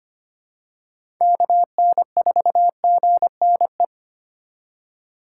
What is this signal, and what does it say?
Morse code, heard as a keyed tone, about 25 wpm, 700 Hz. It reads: KN4GNE
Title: USA Call Signs - 6 Characters - 6x Character Spacing - 25wpm